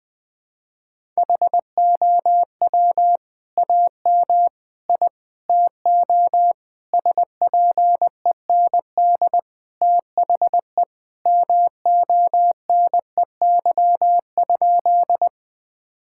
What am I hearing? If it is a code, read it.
Morse code, heard as a keyed tone, about 20 wpm, 700 Hz. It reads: HOW AM I TO SPEND THE MONEY?
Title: Sentences from Top 1000 Words - Review Part 5 - 20wpm